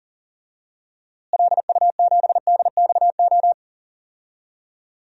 Morse code, heard as a keyed tone, about 38 wpm, 700 Hz. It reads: LU7DXO
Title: DXCC-100 Call Signs - 6 Characters - Repeated 2x - 40wpm